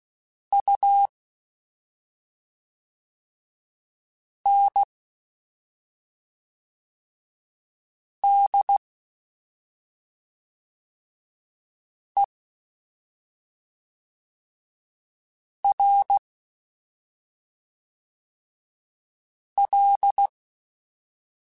Morse code, heard as a keyed tone, about 16 wpm, 800 Hz. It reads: U N D E R L
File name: Special INGs / Special INGs-mp3s / SlowToFaster ing @5